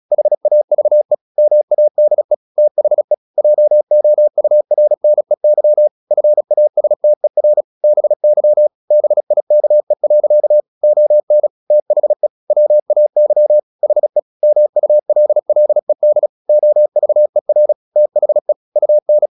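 Morse code, heard as a keyed tone, about 36 wpm, 600 Hz. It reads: HAVE MADE THE JOURNEY FASTER BY BIKE. ON THE WAY HE MULLED OVER THE UN
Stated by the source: Morse practice at 35 wpm from CW med Gnister, sending Ken Follett's novel Never in English.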